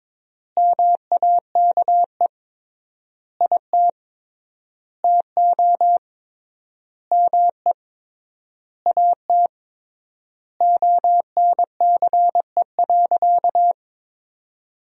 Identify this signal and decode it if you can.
Morse code, heard as a keyed tone, about 22 wpm, 700 Hz. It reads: MAKE IT TO ME AT ONCE.
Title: 3x Word-Spacing - Sentences from Top 500 Words - 22wpm